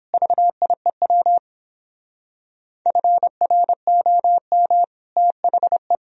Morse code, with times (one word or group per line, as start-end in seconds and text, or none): 0.1-1.4 VIEW
2.9-4.8 FROM
5.2-6.0 THE